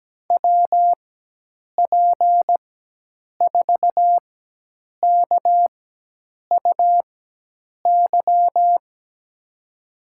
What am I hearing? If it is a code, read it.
Morse code, heard as a keyed tone, about 17 wpm, spaced as for 8 wpm, 700 Hz. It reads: WP4KUY
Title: USA Call Signs - 6 Characters - 4x Character Spacing - 17wpm